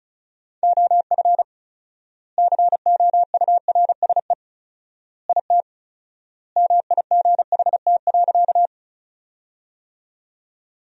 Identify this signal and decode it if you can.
Morse code, heard as a keyed tone, about 36 wpm, 700 Hz. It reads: OF COURSE IT MIGHT.
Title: 4x Word-Spacing - Sentences from Top 500 Words - 35wpm